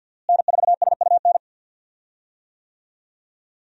Morse code, heard as a keyed tone, about 52 wpm, 700 Hz. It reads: N4SUN
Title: USA Call Signs - 5 Characters - Repeated 3x - 50wpm